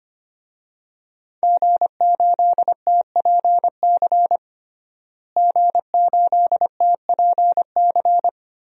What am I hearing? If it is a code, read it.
Morse code, heard as a keyed tone, about 25 wpm, 700 Hz. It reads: G8TPC G8TPC
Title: DXCC-100 Call Signs - 5 Characters - Repeated 3x - 25wpm